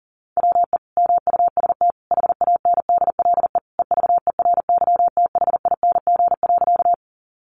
Morse code, heard as a keyed tone, about 40 wpm, 700 Hz. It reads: WE MUST HANDLE EVERYTHING.
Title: Sentences from Top 1100 Words 40wpm